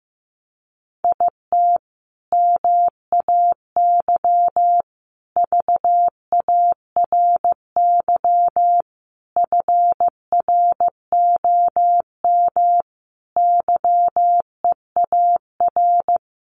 Morse code, heard as a keyed tone, about 15 wpm, 700 Hz. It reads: IT MAY VARY FROM YEAR
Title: Sentences from Top 1200 Words 15wpm